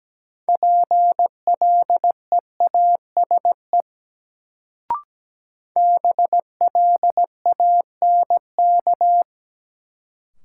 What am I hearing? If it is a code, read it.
Morse code, heard as a keyed tone, about 17 wpm, 700 Hz. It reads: PLEASE BLANK
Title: Sentences from Top 1500 Words - New Word Drill Alt - 17wpm